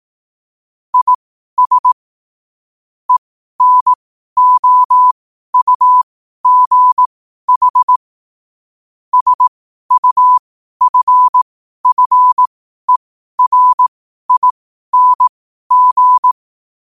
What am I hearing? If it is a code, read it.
Morse code, heard as a keyed tone, about 18 wpm, spaced as for 12 wpm, 1000 Hz. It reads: IS ENOUGH SUFFERING
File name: qotdfiles / 12wpm